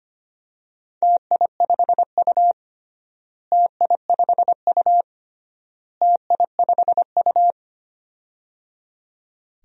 Morse code, heard as a keyed tone, about 25 wpm, 700 Hz. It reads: TI5U TI5U TI5U
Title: DXCC-100 Call Signs - 4 Characters - Repeated 3x - 25wpm